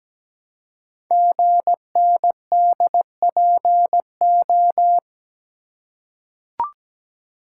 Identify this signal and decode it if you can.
Morse code, heard as a keyed tone, about 17 wpm, 700 Hz. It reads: GNDPO
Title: Code Groups - 5 Letters - 17wpm